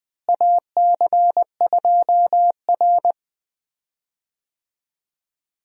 Morse code, heard as a keyed tone, about 20 wpm, 700 Hz. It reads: AC2R
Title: USA Call Signs - 4 Characters - Repeated 2x - 20wpm